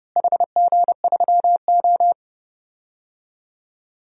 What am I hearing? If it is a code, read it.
Morse code, heard as a keyed tone, about 31 wpm, 700 Hz. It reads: HG3O
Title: DXCC-100 Call Signs - 4 Characters - Repeated 3x - 30wpm